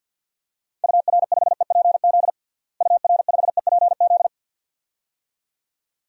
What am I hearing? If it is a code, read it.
Morse code, heard as a keyed tone, about 50 wpm, 700 Hz. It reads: UR5EPZ UR5EPZ
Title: DXCC-100 Call Signs - 6 Characters - Repeated 2x - 50wpm